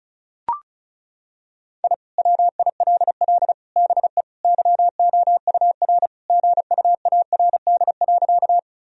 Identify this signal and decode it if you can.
Morse code, heard as a keyed tone, about 36 wpm, 700 Hz. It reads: I WILL BE YOUR GUARD.
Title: Sentences from Top 1000 Words - Review Part 4 - 35wpm